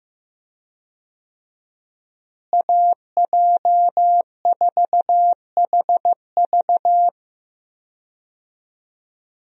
Morse code, heard as a keyed tone, about 15 wpm, 700 Hz. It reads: AJ4HV
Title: USA Call Signs - 5 Characters - Repeated 3x - 15wpm